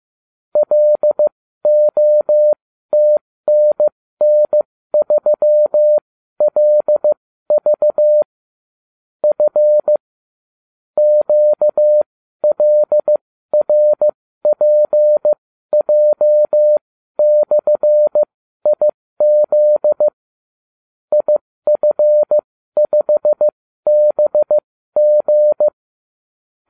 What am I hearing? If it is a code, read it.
Morse code, heard as a keyed tone, about 15 wpm, spaced as for 12 wpm, 600 Hz. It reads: LOTNN3LV F QLRPJ/IZ IF5BG